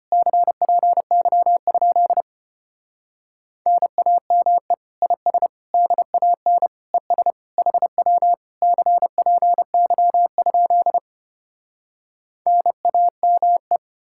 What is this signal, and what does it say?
Morse code, heard as a keyed tone, about 33 wpm, 700 Hz. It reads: CPY? NAME IS DAN ES HW CPY? NAME